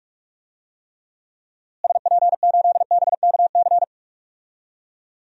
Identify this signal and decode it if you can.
Morse code, heard as a keyed tone, about 43 wpm, 700 Hz. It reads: SP8BKC
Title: DXCC-100 Call Signs - 6 Characters - Repeated 2x - 45wpm